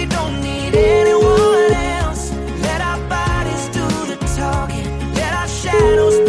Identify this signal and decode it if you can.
Ringback tone (Europe, 425 Hz), with noise.